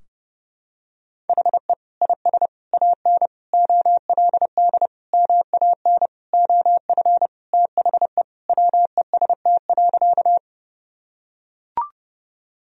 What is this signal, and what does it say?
Morse code, heard as a keyed tone, about 32 wpm, 700 Hz. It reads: HE IS AN OLD MAN OF THE WEST.
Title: Sentences from Top 1000 Words - Review Part 2 - 30wpm